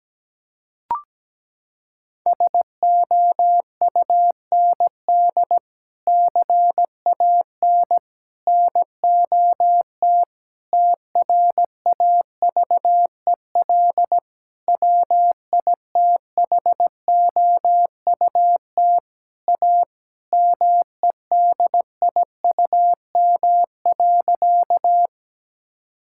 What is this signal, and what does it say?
Morse code, heard as a keyed tone, about 17 wpm, 700 Hz. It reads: SOUND CAN NOT TRAVEL WITHOUT A MEDIUM.
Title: Sentences from Top 1000 Words - Review Part 6 - 17wpm